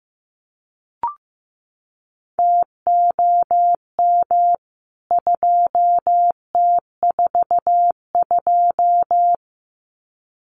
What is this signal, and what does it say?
Morse code, heard as a keyed tone, about 15 wpm, 700 Hz. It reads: TOM 2T42
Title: CWT Exchange 15wpm